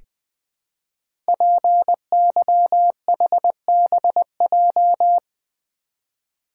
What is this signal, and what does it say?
Morse code, heard as a keyed tone, about 20 wpm, 700 Hz. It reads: PYHBJ
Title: Code Groups - 5 Letters - 20wpm